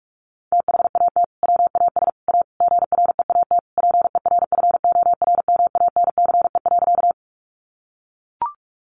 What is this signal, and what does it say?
Morse code, heard as a keyed tone, about 44 wpm, 700 Hz. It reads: THAT WAS A GREAT PERFORMANCE.
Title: Sentences from Top 1000 Words 45wpm